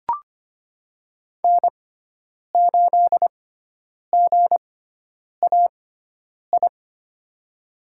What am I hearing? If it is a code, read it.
Morse code, heard as a keyed tone, about 25 wpm, spaced as for 9 wpm, 700 Hz. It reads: N8GAI
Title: USA Call Signs - 5 Characters - 6x Character Spacing - 25wpm